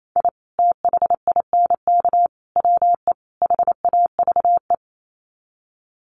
Morse code, heard as a keyed tone, about 28 wpm, 700 Hz. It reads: I THINK WE HAVE
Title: Sets of 4 Words 28wpm